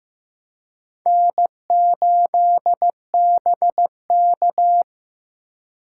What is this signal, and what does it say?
Morse code, heard as a keyed tone, about 15 wpm, 700 Hz. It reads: N8BK
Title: USA Call Signs - 4 Characters - 2x Character Spacing - 15wpm